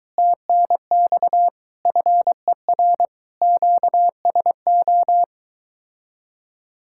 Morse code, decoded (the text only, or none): TNX FER QSO